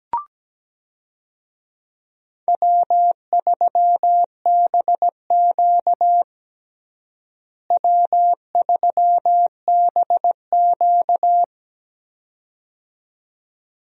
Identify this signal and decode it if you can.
Morse code, heard as a keyed tone, about 17 wpm, 700 Hz. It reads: W3BQ W3BQ
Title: USA Call Signs - 4 Characters - Repeated 2x - 17wpm